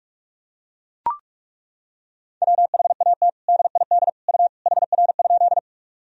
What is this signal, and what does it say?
Morse code, heard as a keyed tone, about 41 wpm, 700 Hz. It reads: WHAT DID U HR?
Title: Sentences from Top 300 Words and Abbreviations 45wpm